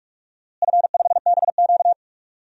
Morse code, heard as a keyed tone, about 46 wpm, 700 Hz. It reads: F5BQ
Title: DXCC-100 Call Signs - 4 Characters - Repeated 2x - 45wpm